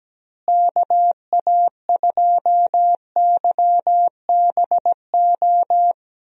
Morse code, heard as a keyed tone, about 17 wpm, 700 Hz. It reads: KA2YBO
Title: USA Call Signs - 6 Characters - Repeated 2x - 17wpm